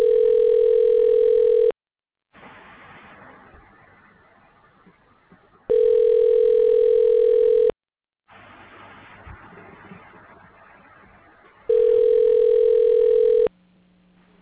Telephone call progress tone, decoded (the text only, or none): ringback tone